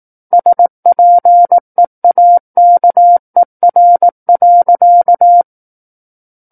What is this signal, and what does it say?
Morse code, heard as a keyed tone, about 18 wpm, 700 Hz. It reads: SPEAKER.
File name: 18 WPM